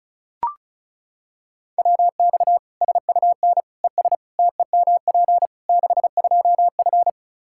Morse code, heard as a keyed tone, about 36 wpm, 700 Hz. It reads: WX SUN ES TEMP 62F
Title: Practice QSO - Exchange 2 - Info - 35wpm